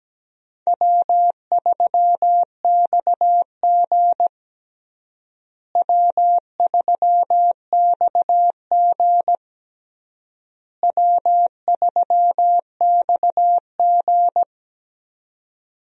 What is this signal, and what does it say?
Morse code, heard as a keyed tone, about 17 wpm, 700 Hz. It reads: W3XG W3XG W3XG
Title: USA Call Signs - 4 Characters - Repeated 3x - 17wpm